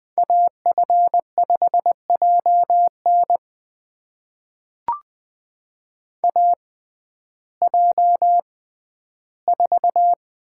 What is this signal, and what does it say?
Morse code, heard as a keyed tone, about 20 wpm, spaced as for 12 wpm, 700 Hz. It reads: AF5JN AJ4
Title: USA Call Signs - 6x Character Spacing - 20wpm